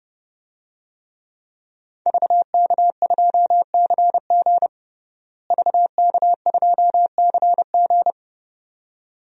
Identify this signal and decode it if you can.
Morse code, heard as a keyed tone, about 32 wpm, 700 Hz. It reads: VK2CG VK2CG
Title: DXCC-100 Call Signs - 5 Characters - Repeated 2x - 30wpm